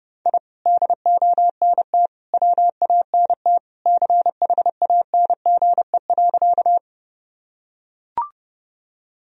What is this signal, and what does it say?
Morse code, heard as a keyed tone, about 32 wpm, 700 Hz. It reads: I DONT WANT CHANGE.
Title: Sentences from Top 1000 Words - Review Part 1 - 30wpm